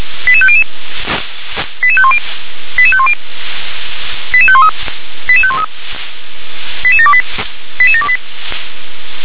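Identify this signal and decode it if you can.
Five-tone selective call (ZVEI), heard as quick runs of five tones, tones of about 70 ms. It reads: 88500 88510 88510 88513 88513 88728 88728
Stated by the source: ZVEI five-tone selective call 88716 received via SDR.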